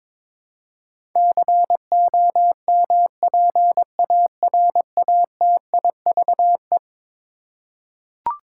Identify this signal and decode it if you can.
Morse code, heard as a keyed tone, about 22 wpm, 700 Hz. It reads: COMPARATIVE
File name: practice / 22wpm / Eleven Letter Words - 22wpm